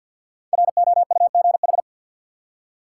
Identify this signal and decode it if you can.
Morse code, heard as a keyed tone, about 47 wpm, 700 Hz. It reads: ROUGH
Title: Five Letter Words - 50wpm